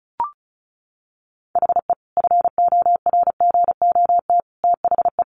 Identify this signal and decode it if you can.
Morse code, heard as a keyed tone, about 34 wpm, 700 Hz. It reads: HE FORGOT THE
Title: Sentences from Top 1400 Words 35wpm